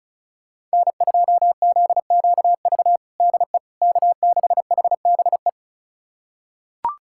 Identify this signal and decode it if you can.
Morse code, heard as a keyed tone, about 34 wpm, 700 Hz. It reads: N2ZQV DE K6HBE
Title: USA Callsign DE Callsign 35wpm